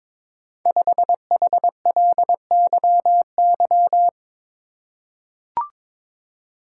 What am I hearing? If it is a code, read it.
Morse code, heard as a keyed tone, about 22 wpm, 700 Hz. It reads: 5HLYY
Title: Code Groups - 5 Letter:Numbers - 22wpm